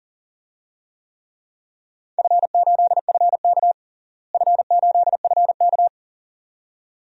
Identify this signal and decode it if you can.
Morse code, heard as a keyed tone, about 40 wpm, 700 Hz. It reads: F8FK F8FK
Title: DXCC-100 Call Signs - 4 Characters - Repeated 2x - 40wpm